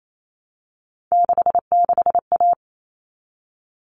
Morse code, heard as a keyed tone, about 28 wpm, 700 Hz. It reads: 66A
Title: All Cut Numbers - 3-Digit 28wpm